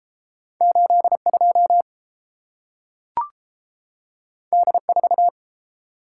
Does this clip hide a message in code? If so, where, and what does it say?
Yes: Morse code from 0.6 to 5.3 s, heard as a keyed tone, about 30 wpm, 700 Hz. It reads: 8 2 D 4